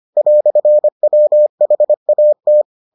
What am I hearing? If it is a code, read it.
Morse code, heard as a keyed tone, about 25 wpm, 600 Hz. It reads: "WHAT